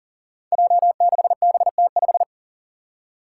Morse code, heard as a keyed tone, about 40 wpm, 700 Hz. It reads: J6BT5
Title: Five Challenging Letter-Number Combinations 40wpm